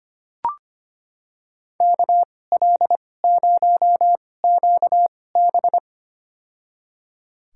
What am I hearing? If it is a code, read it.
Morse code, heard as a keyed tone, about 25 wpm, spaced as for 18 wpm, 700 Hz. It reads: KL0QB
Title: USA Call Signs - 2x Character Spacing - 25wpm